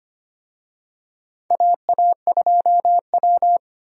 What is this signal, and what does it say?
Morse code, heard as a keyed tone, about 25 wpm, 700 Hz. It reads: AA2W
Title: USA Call Signs - 4 Characters - 25wpm